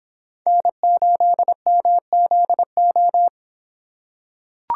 Morse code, heard as a keyed tone, about 25 wpm, 700 Hz. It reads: N8MZO